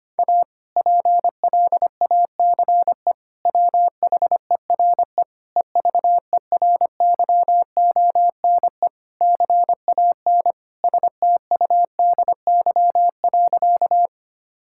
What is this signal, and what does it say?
Morse code, heard as a keyed tone, about 25 wpm, 700 Hz. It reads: A PLACE WHERE EVERYONE CAN STUDY.
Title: Sentences from Top 1000 Words - Review Part 5 - 25wpm